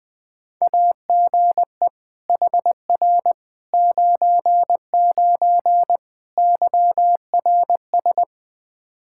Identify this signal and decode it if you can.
Morse code, heard as a keyed tone, about 20 wpm, 700 Hz. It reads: AGE HR 99 YRS
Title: Practice QSO - Exchange 3 - More Info - 20wpm